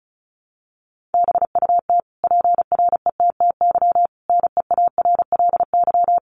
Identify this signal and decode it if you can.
Morse code, heard as a keyed tone, about 34 wpm, 700 Hz. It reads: BUT PRETTY NEARLY
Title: Sentences from Top 1300 Words 35wpm